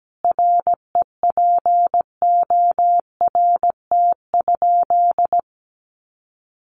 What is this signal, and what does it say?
Morse code, heard as a keyed tone, about 17 wpm, 700 Hz. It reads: REPORT?
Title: Sentences from Top 1200 Words 17wpm